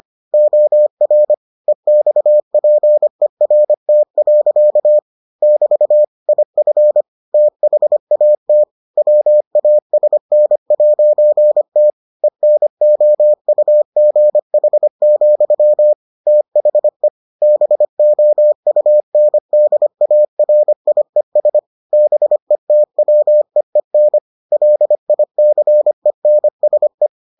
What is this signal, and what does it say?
Morse code, heard as a keyed tone, about 25 wpm, 600 Hz. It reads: OR EXPERT. <BT> IF THAT WASN'T ENOUGH, THE BOUNDARIES BETWEEN LICENSE